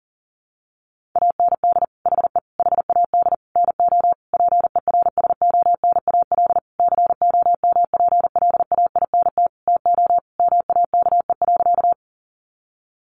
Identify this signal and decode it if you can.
Morse code, heard as a keyed tone, about 40 wpm, 700 Hz. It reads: AND HE HAD NO PERSONAL COMPLAINT TO MAKE.
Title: Sentences from Top 1400 Words 40wpm